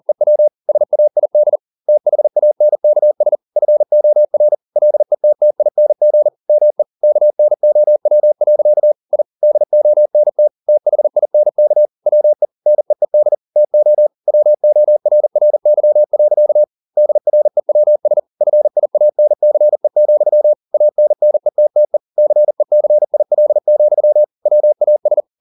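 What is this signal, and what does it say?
Morse code, heard as a keyed tone, about 40 wpm, 600 Hz. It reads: EW SAID THANKS FOR LETTING ME KNOW. I DONT THINK WE NEED TO WORRY. DREWS FIANCE, ANNETTE CECIL, WAS
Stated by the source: Morse code at 40 wpm, practice text, CW med Gnister.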